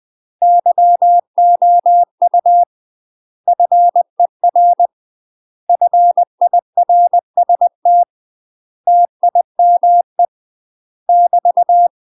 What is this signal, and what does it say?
Morse code, heard as a keyed tone, about 20 wpm, 700 Hz. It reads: YOU FER FIRST TIME <BT>
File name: UK MASTER QSOs TRAINING / 20WPM / UK MASTER QSOs 0026